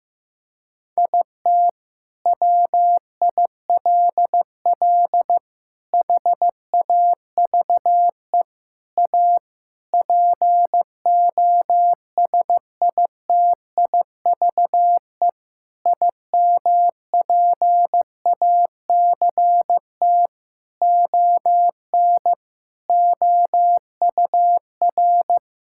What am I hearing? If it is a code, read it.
Morse code, heard as a keyed tone, about 15 wpm, 700 Hz. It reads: IT WILL HAVE A POSITIVE IMPACT ON OUR
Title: Sentences from Top 1000 Words - Review Part 6 - 15wpm